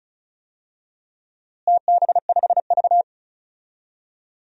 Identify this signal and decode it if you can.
Morse code, heard as a keyed tone, about 36 wpm, 700 Hz. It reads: TB5V
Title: DXCC-100 Call Signs - 4 Characters - Repeated 2x - 35wpm